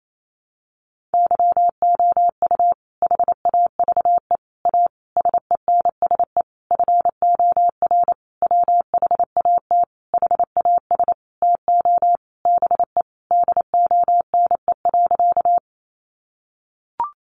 Morse code, heard as a keyed tone, about 28 wpm, 700 Hz. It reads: YOU HAVE A SENSE FOR WHAT HAS TO BE DONE.
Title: Sentences from Top 700 Words 28wpm